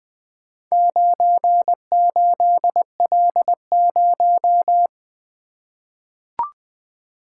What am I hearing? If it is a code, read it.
Morse code, heard as a keyed tone, about 20 wpm, 700 Hz. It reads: 98L0
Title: Four Letter-Number Combinations - 20wpm